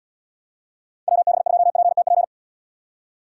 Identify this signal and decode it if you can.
Morse code, heard as a keyed tone, about 71 wpm, 700 Hz. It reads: WB2PTP